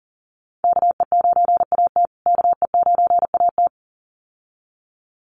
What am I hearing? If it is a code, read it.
Morse code, heard as a keyed tone, about 40 wpm, 700 Hz. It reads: KE9AT KE9AT
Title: USA Call Signs Repeated 2x 40wpm